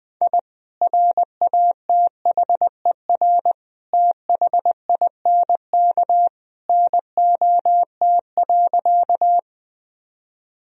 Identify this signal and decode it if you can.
Morse code, heard as a keyed tone, about 20 wpm, 700 Hz. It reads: I RATHER THINK NOT.
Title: Sentences from Top 1000 Words - Review Part 4 - 20wpm